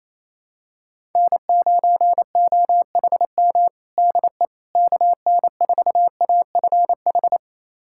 Morse code, heard as a keyed tone, about 27 wpm, 700 Hz. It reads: N9OHM DE KN4AFH
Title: USA Callsign DE Callsign 28wpm